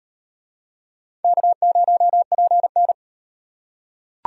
Morse code, heard as a keyed tone, about 38 wpm, 700 Hz. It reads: K0PN